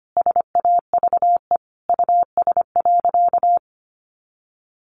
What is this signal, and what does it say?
Morse code, heard as a keyed tone, about 25 wpm, 700 Hz. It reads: SAVE US.
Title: Sentences from Top 1100 Words 25wpm